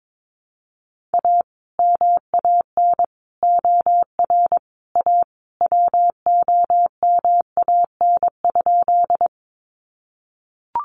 Morse code, heard as a keyed tone, about 22 wpm, 700 Hz. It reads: A MAN OR A WOMAN?
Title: Sentences from Top 1100 Words 22wpm